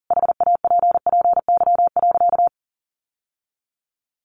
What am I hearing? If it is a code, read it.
Morse code, heard as a keyed tone, about 40 wpm, 700 Hz. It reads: HAPPY.
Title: Sentences from Top 1100 Words 40wpm